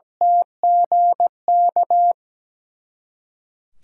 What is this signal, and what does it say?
Morse code, heard as a keyed tone, about 17 wpm, 700 Hz. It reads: TGK
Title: USA Callsign DE Callsign - 6 Characters - 17wpm